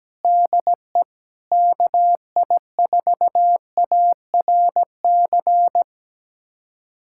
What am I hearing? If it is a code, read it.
Morse code, heard as a keyed tone, about 17 wpm, 700 Hz. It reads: DE KI4ARC
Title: USA Callsign DE Callsign 17wpm